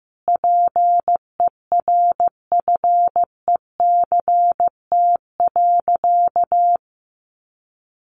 Morse code, heard as a keyed tone, about 15 wpm, 700 Hz. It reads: PERFECT.
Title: Sentences from Top 1300 Words 15wpm